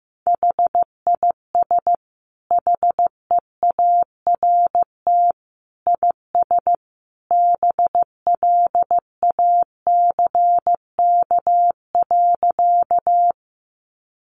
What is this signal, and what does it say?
Morse code, heard as a keyed tone, about 15 wpm, 700 Hz. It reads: HIS HEART IS BLACK.
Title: Sentences from Top 1200 Words 15wpm